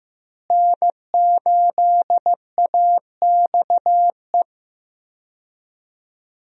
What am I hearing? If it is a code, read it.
Morse code, heard as a keyed tone, about 15 wpm, 700 Hz. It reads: N8AXE
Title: USA Call Signs - 5 Characters - Repeated 3x - 15wpm